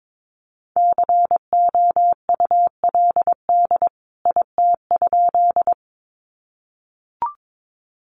Morse code, heard as a keyed tone, about 22 wpm, 700 Hz. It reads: COULD IT?
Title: A Sound Of Thunder 22wpm